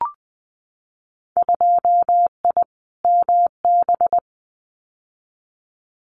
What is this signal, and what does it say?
Morse code, heard as a keyed tone, about 20 wpm, 700 Hz. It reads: 2I MB
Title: Winter Field Day Exchanges Encore 20wpm